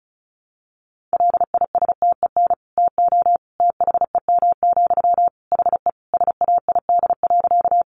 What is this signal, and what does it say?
Morse code, heard as a keyed tone, about 34 wpm, 700 Hz. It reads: LISTEN TO THEM, HE SAID.
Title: Sentences from Top 400 Words 35wpm